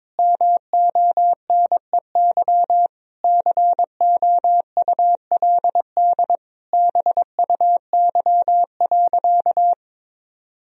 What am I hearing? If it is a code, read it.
Morse code, heard as a keyed tone, about 22 wpm, 700 Hz. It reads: MONEY COULD BUY.
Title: Sentences from Top 1000 Words - Review Part 4 - 22wpm